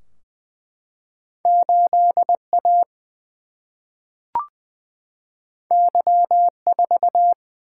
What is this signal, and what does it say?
Morse code, heard as a keyed tone, about 20 wpm, 700 Hz. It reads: 8A Y4
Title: Single-Letter Single-Number 20wpm